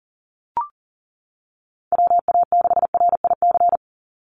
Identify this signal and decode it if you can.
Morse code, heard as a keyed tone, about 38 wpm, 700 Hz. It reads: WA6RIC
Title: NA SOTA Chaser Callsigns 40wpm